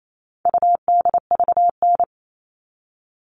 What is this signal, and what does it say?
Morse code, heard as a keyed tone, about 28 wpm, 700 Hz. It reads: UDVN
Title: All Cut Numbers - 4-Digit 28wpm